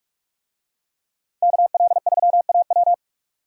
Morse code, heard as a keyed tone, about 44 wpm, 700 Hz. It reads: KL3AW